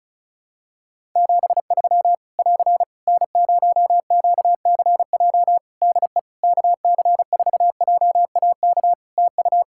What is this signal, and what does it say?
Morse code, heard as a keyed tone, about 36 wpm, 700 Hz. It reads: 73 <AR> N0QCJ DE KC4JAK TU
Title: K1USN SST QSO Practice 35wpm